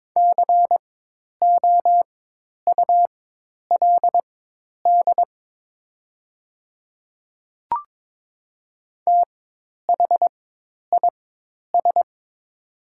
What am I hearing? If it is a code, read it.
Morse code, heard as a keyed tone, about 23 wpm, spaced as for 10 wpm, 700 Hz. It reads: COULD THIS
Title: 4x Character Spacing - Top 100 Words - 22wpm